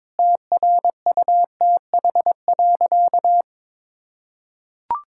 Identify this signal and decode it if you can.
Morse code, heard as a keyed tone, about 22 wpm, 700 Hz. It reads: TRUTH.
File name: practice / 22wpm / Sentences with 8-Word Phrases - 22wpm